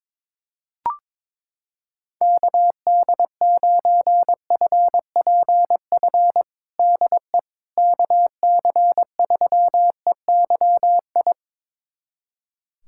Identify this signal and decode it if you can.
Morse code, heard as a keyed tone, about 23 wpm, 700 Hz. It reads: KD9FPF DE KC3EYI
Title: USA Callsign DE Callsign - 6 Characters - 22wpm